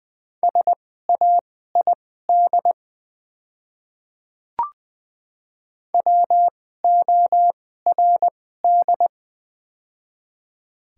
Morse code, heard as a keyed tone, about 20 wpm, spaced as for 14 wpm, 700 Hz. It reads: SAID WORD
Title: 2x Character Spacing - Top 100 Words - 20wpm